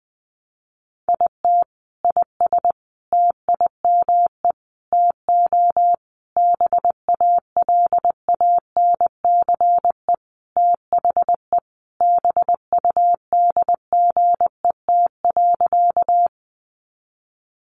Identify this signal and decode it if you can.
Morse code, heard as a keyed tone, about 20 wpm, 700 Hz. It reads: IT IS TIME TO BALANCE THE BUDGET.
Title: Sentences from Top 1200 Words 20wpm